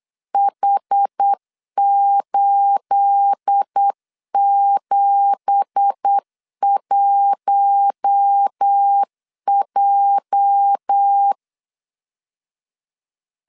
Morse code, heard as a keyed tone, about 8 wpm, 800 Hz. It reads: H871J
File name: CW / DET / 1 9 1500C S C 8wpm 5653